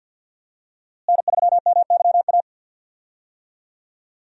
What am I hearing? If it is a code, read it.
Morse code, heard as a keyed tone, about 53 wpm, 700 Hz. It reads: N3MYA
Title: USA Call Signs - 5 Characters - Repeated 2x - 50wpm